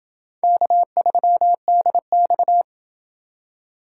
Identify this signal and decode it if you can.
Morse code, heard as a keyed tone, about 28 wpm, 700 Hz. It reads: K3DX